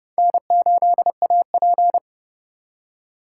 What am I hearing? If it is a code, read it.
Morse code, heard as a keyed tone, about 35 wpm, 700 Hz. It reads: N8AP
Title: Speed Racing - USA 1x2 Call Signs 30wpm